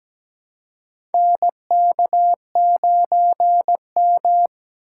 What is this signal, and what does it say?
Morse code, heard as a keyed tone, about 17 wpm, 700 Hz. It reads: NK9M
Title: USA Call Signs - 4 Characters - 3x Character Spacing - 17wpm